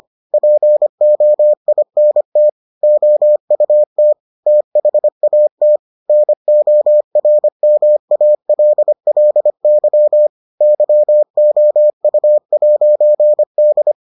Morse code, heard as a keyed tone, about 25 wpm, 600 Hz. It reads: POINT OUT THAT NORMALLY YOU'D